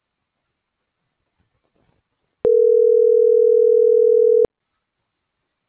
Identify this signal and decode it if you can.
Ringback tone (North America), 440 + 480 Hz.